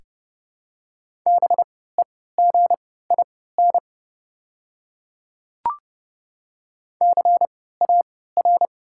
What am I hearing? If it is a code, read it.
Morse code, heard as a keyed tone, about 30 wpm, spaced as for 17 wpm, 690 Hz. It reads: BEGIN CAR
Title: 3x Character Spacing - Top 300 Words - 30wpm